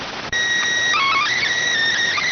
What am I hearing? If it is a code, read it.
SSTV (slow-scan television), PD 290.